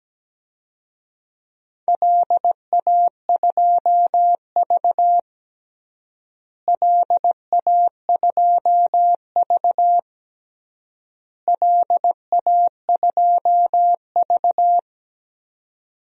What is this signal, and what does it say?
Morse code, heard as a keyed tone, about 17 wpm, 700 Hz. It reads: LA2V LA2V LA2V
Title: DXCC-100 Call Signs - 4 Characters - Repeated 3x - 17wpm